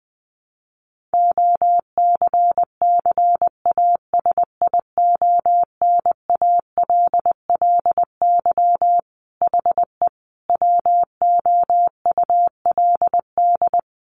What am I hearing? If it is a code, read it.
Morse code, heard as a keyed tone, about 20 wpm, 700 Hz. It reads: OCCASIONALLY HE WOULD
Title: Sentences from Top 1300 Words 20wpm